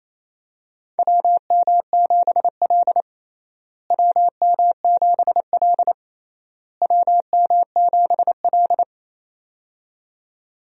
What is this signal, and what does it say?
Morse code, heard as a keyed tone, about 28 wpm, 700 Hz. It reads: WM7L WM7L WM7L
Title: USA Call Signs - 4 Characters - Repeated 3x - 28wpm